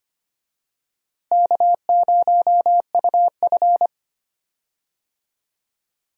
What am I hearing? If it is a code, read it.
Morse code, heard as a keyed tone, about 25 wpm, 700 Hz. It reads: K0UF
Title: USA Call Signs - 4 Characters - Repeated 3x - 25wpm